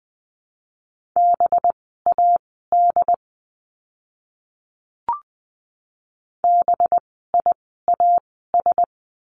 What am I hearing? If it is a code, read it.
Morse code, heard as a keyed tone, about 20 wpm, spaced as for 14 wpm, 700 Hz. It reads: BAD BIAS
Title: 185 Lesson 30f - Letter b - Words 20wpm